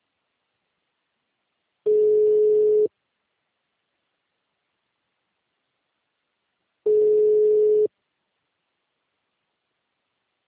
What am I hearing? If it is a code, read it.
Ringback tone (Europe, 425 Hz).